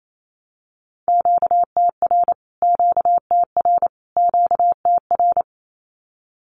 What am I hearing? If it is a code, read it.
Morse code, heard as a keyed tone, about 28 wpm, 700 Hz. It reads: QTR QTR QTR
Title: Q-Codes Repeated 3x 28wpm